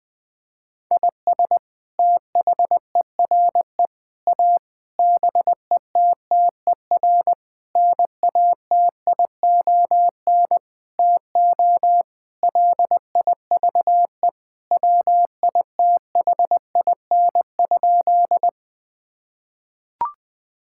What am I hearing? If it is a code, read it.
Morse code, heard as a keyed tone, about 20 wpm, 700 Hz. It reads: IS THERE A BETTER NATION TO LIVE WITHIN?
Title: Sentences from Top 1000 Words - Review Part 6 - 20wpm